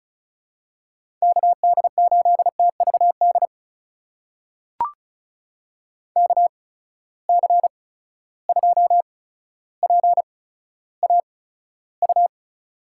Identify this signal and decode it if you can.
Morse code, heard as a keyed tone, about 36 wpm, spaced as for 15 wpm, 700 Hz. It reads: KD8TVD KC2PAU
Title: USA Call Signs - 6 Characters - 8x Character Spacing - 35wpm